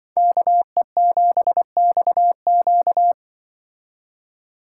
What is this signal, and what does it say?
Morse code, heard as a keyed tone, about 23 wpm, 700 Hz. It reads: KE7XQ
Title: Speed Racing - USA 2x2 Call Signs 20wpm